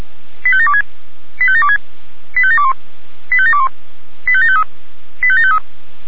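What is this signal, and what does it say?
Five-tone selective call (ZVEI), heard as quick runs of five tones, tones of about 70 ms. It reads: 85726 85726 85721 85721 85753 85753
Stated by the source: ZVEI five-tone selective call received with an SDR.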